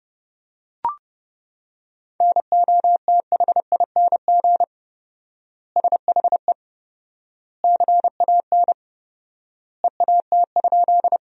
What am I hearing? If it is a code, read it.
Morse code, heard as a keyed tone, about 30 wpm, 700 Hz. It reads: NOTHING SHE CAN EAT?
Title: 4x Word-Spacing - Sentences from Top 500 Words - 30wpm